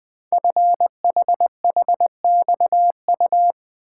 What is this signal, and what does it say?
Morse code, heard as a keyed tone, about 20 wpm, 700 Hz. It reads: FHHXU
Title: Code Groups - 5 Letters - 20wpm